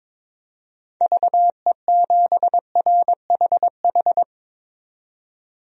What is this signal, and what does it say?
Morse code, heard as a keyed tone, about 22 wpm, 700 Hz. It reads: VE7RHH
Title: DXCC-100 Call Signs - 6 Characters - 22wpm